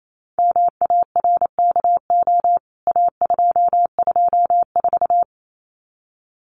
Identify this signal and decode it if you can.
Morse code, heard as a keyed tone, about 28 wpm, 700 Hz. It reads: MARKO A224
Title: CWT Exchange 28wpm